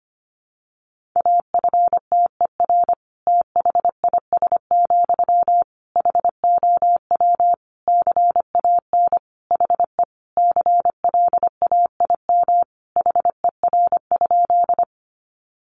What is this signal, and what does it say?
Morse code, heard as a keyed tone, about 25 wpm, 700 Hz. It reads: AFTER THIS, HOW CAN HE CLAIM HER?
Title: Sentences from Top 1500 Words 25wpm